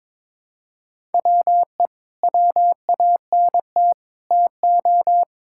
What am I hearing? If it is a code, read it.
Morse code, heard as a keyed tone, about 22 wpm, 700 Hz. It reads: WE WANT TO
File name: Sentences with 9-Word Phrases - 22wpm